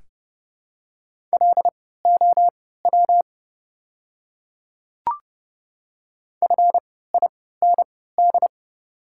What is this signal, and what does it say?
Morse code, heard as a keyed tone, about 30 wpm, spaced as for 17 wpm, 690 Hz. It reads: LOW FIND
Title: 3x Character Spacing - Top 200 Words - 30wpm